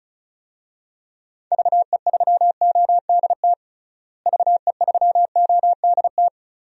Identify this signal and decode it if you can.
Morse code, heard as a keyed tone, about 36 wpm, 700 Hz. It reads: VE3ODT VE3ODT
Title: DXCC-100 Call Signs - 6 Characters - Repeated 3x - 35wpm